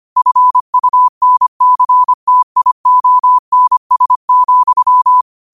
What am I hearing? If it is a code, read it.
Morse code, heard as a keyed tone, about 25 wpm, 1000 Hz. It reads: FUNCTIONS,
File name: 25wpm